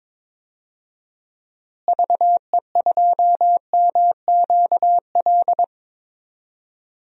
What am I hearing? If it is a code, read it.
Morse code, heard as a keyed tone, about 22 wpm, 700 Hz. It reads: VE2MQL
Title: DXCC-100 Call Signs - 6 Characters - Repeated 3x - 22wpm